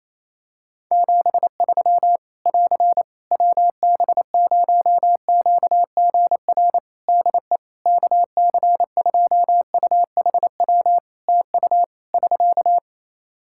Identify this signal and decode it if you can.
Morse code, heard as a keyed tone, about 28 wpm, 700 Hz. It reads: 73 <AR> WB0QGR DE KC2UHW TU <SK>
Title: Practice QSO - Exchange 4 - Ending - 28wpm